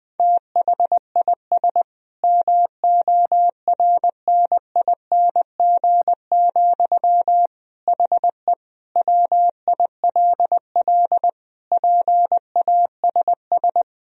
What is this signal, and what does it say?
Morse code, heard as a keyed tone, about 21 wpm, 700 Hz. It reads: THIS MORNING, HE WILL PASS
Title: Sentences from Top 1000 Words - Review Part 1 - 20wpm